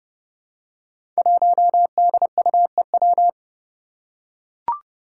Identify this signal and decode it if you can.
Morse code, heard as a keyed tone, about 32 wpm, 700 Hz. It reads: 1DUEW
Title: Five Challenging Letter-Number Combinations 30wpm